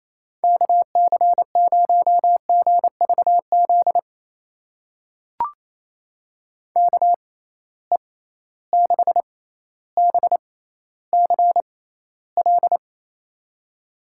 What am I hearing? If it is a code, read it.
Morse code, heard as a keyed tone, about 28 wpm, spaced as for 14 wpm, 700 Hz. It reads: KC0GVZ KE6BCL
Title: USA Call Signs - 6 Characters - 6x Character Spacing - 28wpm